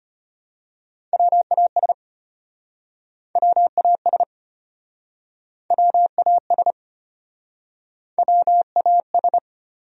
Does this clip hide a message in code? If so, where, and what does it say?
Yes: Morse code from 1.1 to 9.4 s, heard as a keyed tone, about 34 wpm, 700 Hz. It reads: WAS WAS WAS WAS